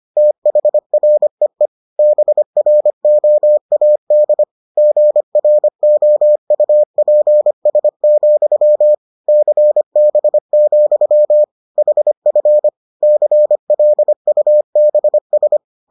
Morse code, heard as a keyed tone, about 25 wpm, 600 Hz. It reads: THREE BROAD GROUPS, CB, HF CLUBS